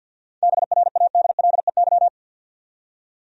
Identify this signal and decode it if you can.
Morse code, heard as a keyed tone, about 52 wpm, 700 Hz. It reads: BRADLEY
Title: USA Common Names - 50wpm